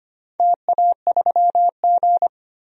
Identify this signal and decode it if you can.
Morse code, heard as a keyed tone, about 25 wpm, 700 Hz. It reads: TA3G
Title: DXCC-100 Call Signs - 4 Characters - 25wpm